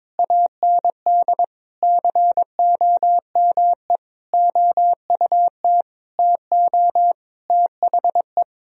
Morse code, heard as a keyed tone, about 21 wpm, 700 Hz. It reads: AND COME OUT TO THE
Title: Sentences from Top 1000 Words - Review Part 1 - 22wpm